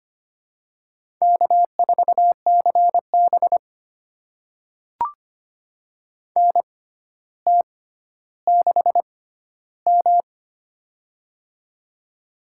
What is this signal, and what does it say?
Morse code, heard as a keyed tone, about 25 wpm, spaced as for 13 wpm, 700 Hz. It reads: K4CB NT6M
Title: USA Call Signs - 4 Characters - 6x Character Spacing - 25wpm